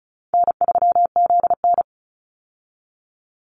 Morse code, heard as a keyed tone, about 34 wpm, 700 Hz. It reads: N3ZN
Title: Easy Call Signs Repeated 3x - 35wpm